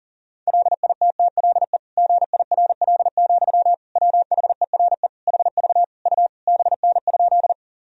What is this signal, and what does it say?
Morse code, heard as a keyed tone, about 40 wpm, 700 Hz. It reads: LITTLE GIRL, WHERE HV U BN?
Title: Sentences from Top 400 Words and Abbreviations 40wpm